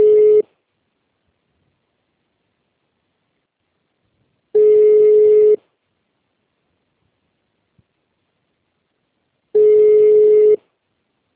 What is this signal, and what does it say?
Ringback tone (Europe, 425 Hz).